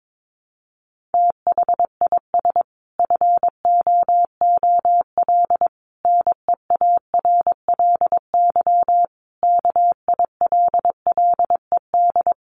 Morse code, heard as a keyed tone, about 22 wpm, 700 Hz. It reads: THIS FOOL NEARLY KILLED
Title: A Sound Of Thunder 22wpm